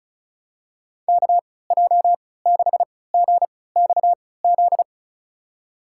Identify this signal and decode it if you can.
Morse code, heard as a keyed tone, about 34 wpm, spaced as for 20 wpm, 700 Hz. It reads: KJ6GXZ
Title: USA Call Signs - 6 Characters - 3x Character Spacing - 35wpm